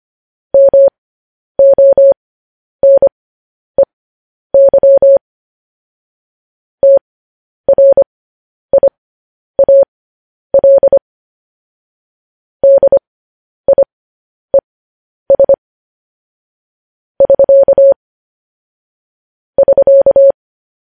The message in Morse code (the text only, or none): MONEY TRIAL DIES <SK> <SK>